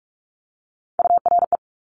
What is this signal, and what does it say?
Morse code, heard as a keyed tone, about 46 wpm, 700 Hz. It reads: URE
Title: Top Three Letter Combinations in Words 45wpm